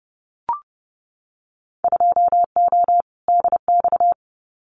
Morse code, heard as a keyed tone, about 30 wpm, 700 Hz. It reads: 2O DX
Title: Winter Field Day Exchanges Encore 30wpm